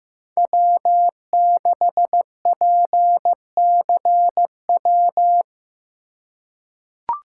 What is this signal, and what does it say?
Morse code, heard as a keyed tone, about 15 wpm, 700 Hz. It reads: W6PCW
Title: USA Call Signs - 5 Characters - 2x Character Spacing - 15wpm